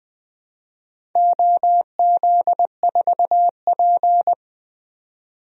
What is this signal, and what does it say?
Morse code, heard as a keyed tone, about 21 wpm, 700 Hz. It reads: OZ4P
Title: DXCC-100 Call Signs - 4 Characters - 20wpm